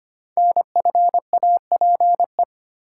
Morse code, heard as a keyed tone, about 25 wpm, 700 Hz. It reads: NFAPE
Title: Code Groups - 5 Letters - 25wpm